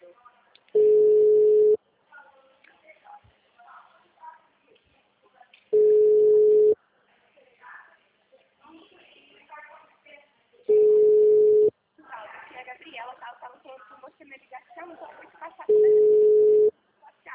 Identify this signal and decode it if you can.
Ringback tone (Europe, 425 Hz), with speech.